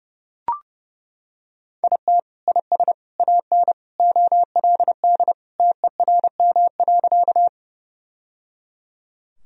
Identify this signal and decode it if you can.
Morse code, heard as a keyed tone, about 30 wpm, 700 Hz. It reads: IT IS AN OLD TERM.
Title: Sentences from Top 1000 Words - Review Part 4 - 30wpm